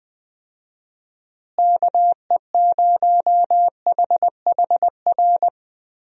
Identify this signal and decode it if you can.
Morse code, heard as a keyed tone, about 20 wpm, 700 Hz. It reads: KE0HHR
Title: USA Call Signs - 6 Characters - Repeated 3x - 20wpm